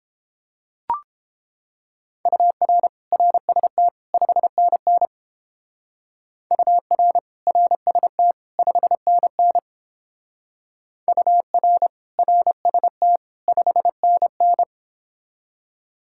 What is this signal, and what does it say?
Morse code, heard as a keyed tone, about 28 wpm, 700 Hz. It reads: UR RST 5NN UR RST 5NN UR RST 5NN